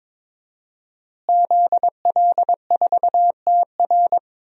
Morse code, heard as a keyed tone, about 21 wpm, 700 Hz. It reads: ZL4TR
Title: DXCC-100 Call Signs - 5 Characters - Repeated 2x - 22wpm